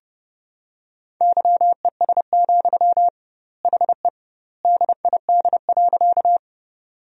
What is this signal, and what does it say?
Morse code, heard as a keyed tone, about 30 wpm, 700 Hz. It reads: YES, HE DID.
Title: 2x Word-Spacing - Sentences from Top 500 Words - 30wpm